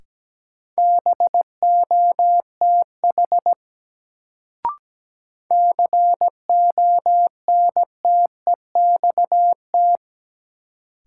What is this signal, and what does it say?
Morse code, heard as a keyed tone, about 17 wpm, 700 Hz. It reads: BOTH CONTEXT
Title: Sets of 4 Words - New Word Drill - 17wpm